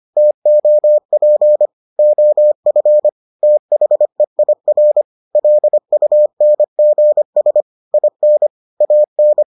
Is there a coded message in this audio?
Yes: Morse code from 0.2 to 9.4 s, heard as a keyed tone, about 25 wpm, 600 Hz. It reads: TOP OF THEIR LUNGS IN AN